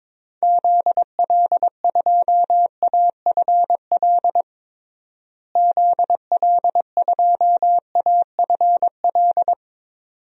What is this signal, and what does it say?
Morse code, heard as a keyed tone, about 22 wpm, 700 Hz. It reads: ZL2AFL ZL2AFL
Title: DXCC-100 Call Signs - 6 Characters - Repeated 2x - 22wpm